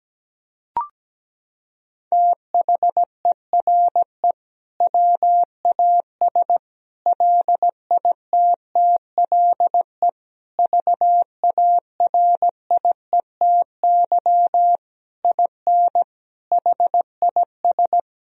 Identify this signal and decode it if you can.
Morse code, heard as a keyed tone, about 17 wpm, 700 Hz. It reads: THERE WAS LITTLE VARIETY IN HIS
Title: Sentences from Top 1000 Words - Review Part 4 - 17wpm